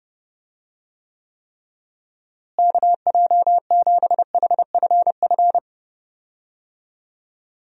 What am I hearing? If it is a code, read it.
Morse code, heard as a keyed tone, about 30 wpm, 700 Hz. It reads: KJ7HFF
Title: USA Call Signs - 6 Characters - Repeated 3x - 30wpm